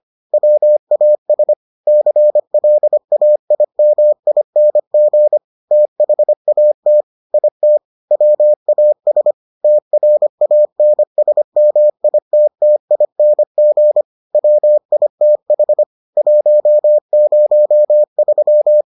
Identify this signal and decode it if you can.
Morse code, heard as a keyed tone, about 25 wpm, 600 Hz. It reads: WAS CLAIMING THAT IT WAS TRANSMITTING WITH 103